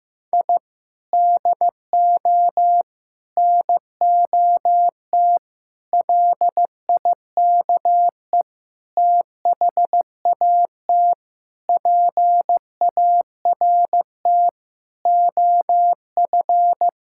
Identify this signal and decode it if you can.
Morse code, heard as a keyed tone, about 14 wpm, 700 Hz. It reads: I DO NOT LIKE THAT PART OF
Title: Sentences from Top 1000 Words - Review Part 6 - 15wpm